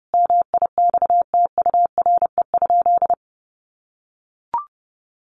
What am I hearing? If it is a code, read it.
Morse code, heard as a keyed tone, about 30 wpm, 700 Hz. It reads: MIXTURE?
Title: Sentences from Top 1500 Words 30wpm